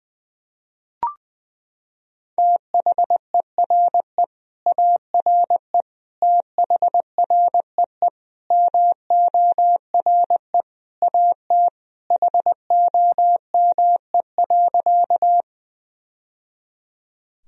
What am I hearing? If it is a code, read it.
Morse code, heard as a keyed tone, about 20 wpm, 700 Hz. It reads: THERE ARE THREE MORE AT HOME.
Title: Sentences from Top 1000 Words - Review Part 1 - 20wpm